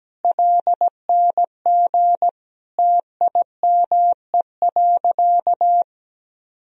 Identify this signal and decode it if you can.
Morse code, heard as a keyed tone, about 17 wpm, 700 Hz. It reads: LNG TIME.
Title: Sentences from Top 400 Words and Abbreviations 17wpm